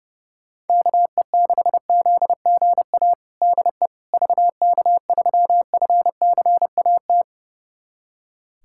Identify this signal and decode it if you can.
Morse code, heard as a keyed tone, about 30 wpm, 700 Hz. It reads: KE6ZGA DE VK3FCAT
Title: USA Callsign DE Callsign 30wpm